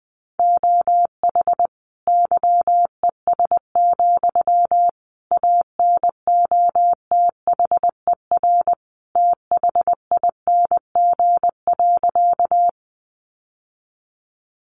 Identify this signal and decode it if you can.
Morse code, heard as a keyed tone, about 20 wpm, 700 Hz. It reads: OH YES, ANOTHER THING.
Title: Sentences from Top 600 Words 20wpm